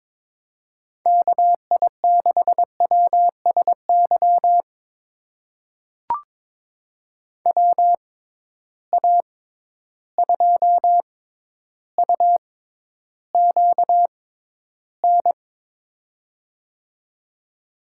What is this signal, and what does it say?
Morse code, heard as a keyed tone, about 22 wpm, spaced as for 11 wpm, 700 Hz. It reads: KI6WSY WA2UQN
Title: USA Call Signs - 6 Characters - 6x Character Spacing - 22wpm